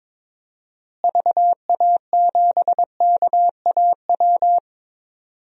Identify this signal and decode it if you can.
Morse code, heard as a keyed tone, about 22 wpm, 700 Hz. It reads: VA7KAW
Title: DXCC-100 Call Signs - 6 Characters - Repeated 2x - 22wpm